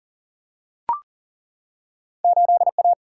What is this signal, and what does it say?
Morse code, heard as a keyed tone, about 37 wpm, 700 Hz. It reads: 8A
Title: Single-Letter Single-Number 40wpm